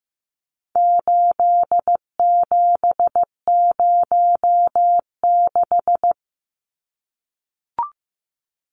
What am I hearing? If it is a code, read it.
Morse code, heard as a keyed tone, about 15 wpm, 700 Hz. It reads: 8706
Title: Numbers - 4-Digit 15wpm